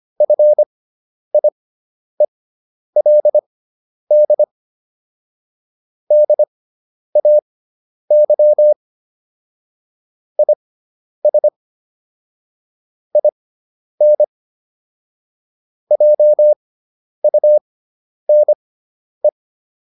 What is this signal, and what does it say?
Morse code, heard as a keyed tone, about 25 wpm, spaced as for 10 wpm, 600 Hz. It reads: FIELD DAY IS IN JUNE